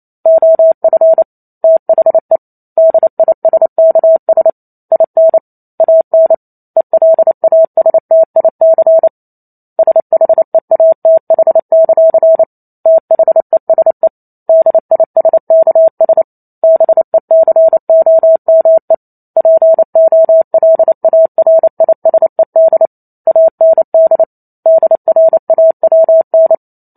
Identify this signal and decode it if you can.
Morse code, heard as a keyed tone, about 28 wpm, 650 Hz. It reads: OF THE DISKS IN AN ELASTIC SHEATH; THESE DISKS BECOME POLARISED AND DRAWN
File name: Chapter 2 - 28 WPM